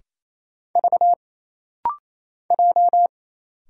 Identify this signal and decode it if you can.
Morse code, heard as a keyed tone, about 27 wpm, 700 Hz. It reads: V J
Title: Single Letter-Number Alt - Rapid-Fire 28wpm